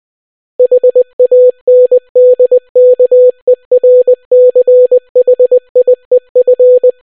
Morse code, heard as a keyed tone, about 20 wpm, 500 Hz. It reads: HANDKERCHIEF